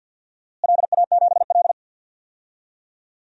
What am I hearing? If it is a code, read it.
Morse code, heard as a keyed tone, about 49 wpm, 700 Hz. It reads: LA7L
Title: DXCC-100 Call Signs - 4 Characters - Repeated 2x - 50wpm